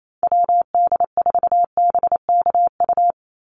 Morse code, heard as a keyed tone, about 28 wpm, 700 Hz. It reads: WD4BKU
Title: USA Call Signs 28wpm